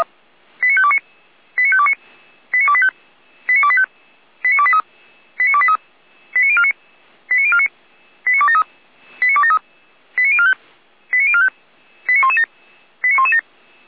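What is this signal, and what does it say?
Five-tone selective call (ZVEI), heard as quick runs of five tones, tones of about 70 ms. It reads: 89529 89529 89285 89285 89283 89283 89049 89049 89273 89273 89045 89045 89117 89117
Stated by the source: ZVEI five-tone call 89305 received on an SDR by Feuerwehr Orlamünde.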